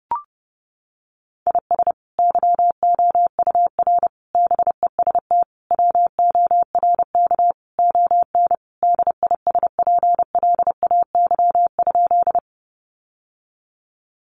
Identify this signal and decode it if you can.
Morse code, heard as a keyed tone, about 30 wpm, 700 Hz. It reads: IS YOUR BEST WORK ON DISPLAY?
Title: Sentences from Top 1200 Words 30wpm